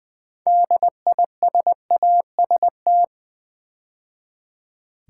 Morse code, heard as a keyed tone, about 20 wpm, 700 Hz. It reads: DISAST
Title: Build A Word - 2 Letters At A Time - 8-Letter Words - 20wpm